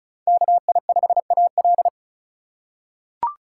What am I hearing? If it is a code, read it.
Morse code, heard as a keyed tone, about 34 wpm, 700 Hz. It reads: KI5AL
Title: K1USN SST QSO Practice - 35wpm